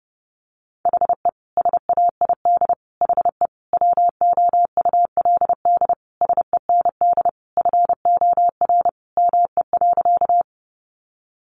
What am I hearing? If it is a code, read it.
Morse code, heard as a keyed tone, about 30 wpm, 700 Hz. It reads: HE SAID HE WOULD SEND FOR ME.
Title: Sentences from Top 1500 Words 30wpm